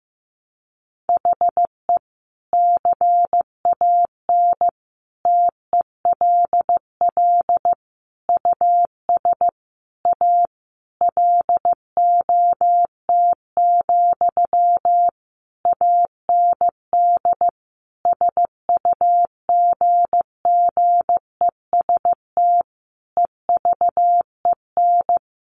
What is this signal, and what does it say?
Morse code, heard as a keyed tone, about 15 wpm, 700 Hz. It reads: HE CAN TELL US A LOT, AND SUGGEST EVEN
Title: Sentences from Top 1300 Words 15wpm